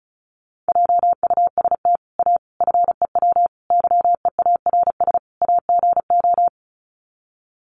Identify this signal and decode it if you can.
Morse code, heard as a keyed tone, about 34 wpm, 700 Hz. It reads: JUST A FEW YEARS AGO
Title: Sets of 5 Words Encore 35wpm